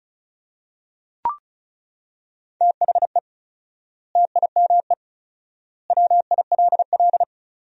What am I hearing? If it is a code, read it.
Morse code, heard as a keyed tone, about 36 wpm, 700 Hz. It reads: THE TIME WILL
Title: 4x Word-Spacing - Sentences from Top 100 Words - 35wpm